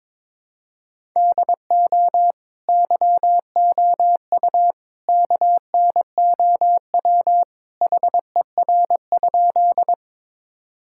Morse code, heard as a keyed tone, about 22 wpm, 700 Hz. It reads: DO YOU KNOW HER?
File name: practice / 22wpm / Sentences with 3-Word Phrases - 22wpm